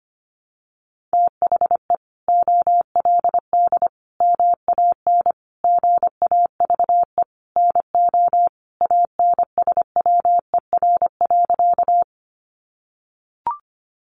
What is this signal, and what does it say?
Morse code, heard as a keyed tone, about 25 wpm, 700 Hz. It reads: THE OLD MAN GAVE NO ANSWER.
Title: Sentences from Top 500 Words 25wpm